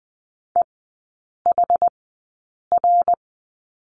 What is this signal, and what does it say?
Morse code, heard as a keyed tone, about 20 wpm, spaced as for 8 wpm, 700 Hz. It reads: EHR
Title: 107 Lesson 18e - Letter m - Three Characters 20wpm